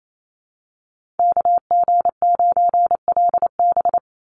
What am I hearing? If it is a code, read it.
Morse code, heard as a keyed tone, about 28 wpm, 700 Hz. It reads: KG9LB
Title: Call Signs Repeated 3x - 28wpm